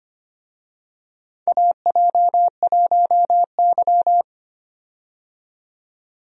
Morse code, heard as a keyed tone, about 25 wpm, 700 Hz. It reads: AJ1Y
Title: USA Call Signs - 4 Characters - 25wpm